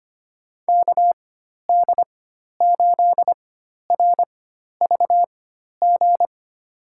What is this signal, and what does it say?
Morse code, heard as a keyed tone, about 25 wpm, spaced as for 12 wpm, 700 Hz. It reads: KD8RVG
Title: USA Call Signs - 6 Characters - 4x Character Spacing - 25wpm